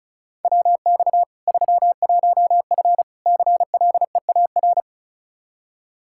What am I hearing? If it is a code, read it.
Morse code, heard as a keyed tone, about 36 wpm, 700 Hz. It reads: WX 31F CLEAR
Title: Practice QSO - Exchange 2 - Info - 35wpm